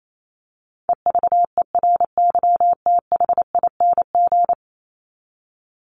Morse code, heard as a keyed tone, about 28 wpm, 700 Hz. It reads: EVERYTHING
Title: Top 501-750 Words 28wpm